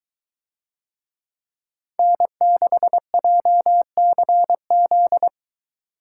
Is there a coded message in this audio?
Yes: Morse code from 2.0 to 5.3 s, heard as a keyed tone, about 21 wpm, 700 Hz. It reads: N6JCZ